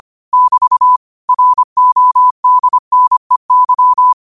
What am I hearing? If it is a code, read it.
Morse code, heard as a keyed tone, about 25 wpm, 1000 Hz. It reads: <BT> RODNEY
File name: qotdfiles / 25wpm